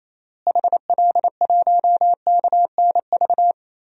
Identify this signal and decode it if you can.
Morse code, heard as a keyed tone, about 28 wpm, 700 Hz. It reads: HL1KNV
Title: DXCC-100 Call Signs - 6 Characters - Repeated 3x - 28wpm